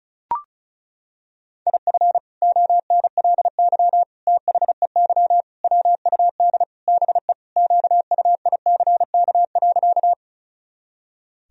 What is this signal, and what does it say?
Morse code, heard as a keyed tone, about 34 wpm, 700 Hz. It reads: IF ONLY THEY WUD BE QUICK.
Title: Sentences from Top 500 Words and Abbreviations 35wpm